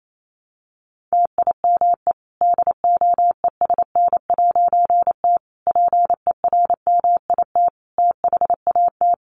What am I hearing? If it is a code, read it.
Morse code, heard as a keyed tone, about 28 wpm, 700 Hz. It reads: TIME DOESN'T PERMIT THAT
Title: A Sound Of Thunder 28wpm